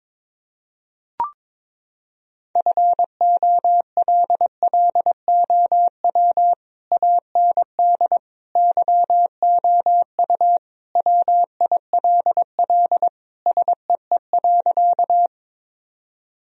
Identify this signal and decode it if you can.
Morse code, heard as a keyed tone, about 22 wpm, 700 Hz. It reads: FOLLOW AND YOU WILL SEE.
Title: Sentences from Top 1000 Words - Review Part 1 - 22wpm